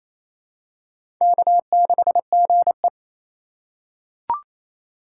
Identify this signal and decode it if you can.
Morse code, heard as a keyed tone, about 28 wpm, 700 Hz. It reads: K6GE
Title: USA Call Signs - 4 Characters - 4x Character Spacing - 28wpm